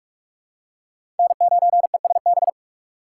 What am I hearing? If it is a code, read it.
Morse code, heard as a keyed tone, about 43 wpm, 700 Hz. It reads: N9ESB
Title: USA Call Signs - 5 Characters - 4x Character Spacing - 45wpm